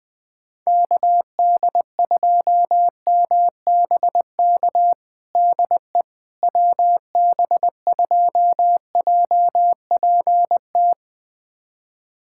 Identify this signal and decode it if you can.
Morse code, heard as a keyed tone, about 21 wpm, 700 Hz. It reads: KD2MBK DE WB2JPT
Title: USA Callsign DE Callsign 20wpm